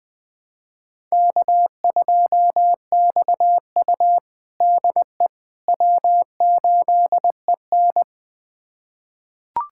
Morse code, heard as a keyed tone, about 20 wpm, 700 Hz. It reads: K2XU DE W8EN
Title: USA Callsign DE Callsign - 4 Characters - 20wpm